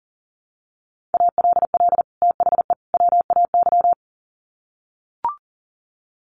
Morse code, heard as a keyed tone, about 40 wpm, 700 Hz. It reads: ALL THE WAY
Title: Sets of 3 Words 40wpm